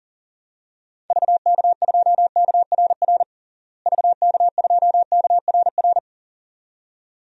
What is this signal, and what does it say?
Morse code, heard as a keyed tone, about 40 wpm, 700 Hz. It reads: VK2KRR VK2KRR
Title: DXCC-100 Call Signs - 6 Characters - Repeated 2x - 40wpm